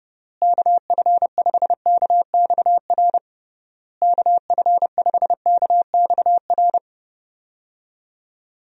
Morse code, heard as a keyed tone, about 31 wpm, 700 Hz. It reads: KF5KXR KF5KXR
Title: USA Call Signs - 6 Characters - Repeated 3x - 30wpm